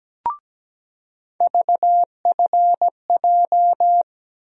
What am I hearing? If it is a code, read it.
Morse code, heard as a keyed tone, about 17 wpm, 700 Hz. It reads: VFJ
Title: USA Call Sign Postfixes - 3 Letters 17wpm